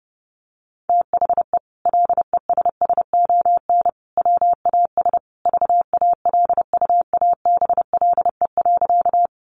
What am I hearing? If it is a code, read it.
Morse code, heard as a keyed tone, about 30 wpm, 700 Hz. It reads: THE LESSON WAS VALUABLE.
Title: Sentences from Top 1200 Words 30wpm